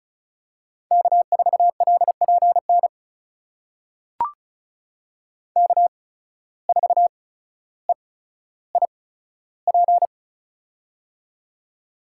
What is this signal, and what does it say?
Morse code, heard as a keyed tone, about 36 wpm, spaced as for 15 wpm, 700 Hz. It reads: K4LPN K4EIP
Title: USA Call Signs - 5 Characters - 8x Character Spacing - 35wpm